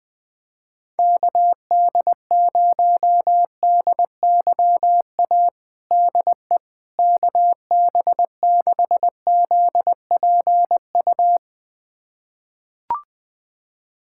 Morse code, heard as a keyed tone, about 21 wpm, 700 Hz. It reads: KD0DYA DE KB6ZPU
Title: USA Callsign DE Callsign 20wpm